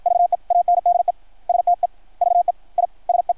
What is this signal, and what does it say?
Morse code, heard as a keyed tone, about 53 wpm, spaced as for 45 wpm, 690 Hz. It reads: VE ANDE STEVE I SE
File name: archived / CRSnet